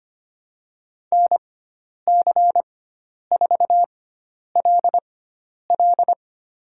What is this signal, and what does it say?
Morse code, heard as a keyed tone, about 25 wpm, spaced as for 10 wpm, 700 Hz. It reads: NC4LL
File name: Lesson 16g - Letter c - Call Signs 25wpm